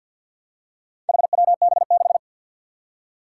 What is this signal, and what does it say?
Morse code, heard as a keyed tone, about 47 wpm, 700 Hz. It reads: HWB6
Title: Four Challenging Letter-Number Combinations 50wpm